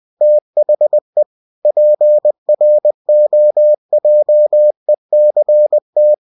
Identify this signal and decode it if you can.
Morse code, heard as a keyed tone, about 20 wpm, 600 Hz. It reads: THE PROJECT
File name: cw-books / winnie-the-pooh / winnie-20wpm-page-0120